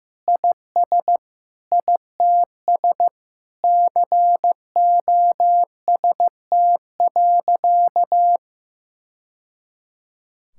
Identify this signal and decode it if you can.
Morse code, heard as a keyed tone, about 16 wpm, 700 Hz. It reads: IS ITS COST.
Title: Sentences from Top 1000 Words - Review Part 6 - 15wpm